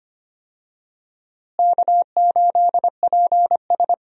Morse code, heard as a keyed tone, about 25 wpm, 700 Hz. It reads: K8PS
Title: USA Call Signs - 4 Characters - Repeated 3x - 25wpm